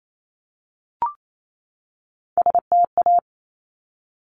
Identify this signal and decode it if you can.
Morse code, heard as a keyed tone, about 28 wpm, 700 Hz. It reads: STA
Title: Top Three Letter Combinations in Words 28wpm